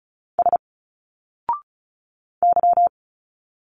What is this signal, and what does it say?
Morse code, heard as a keyed tone, about 34 wpm, 700 Hz. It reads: S Y
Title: Single Letters 35wpm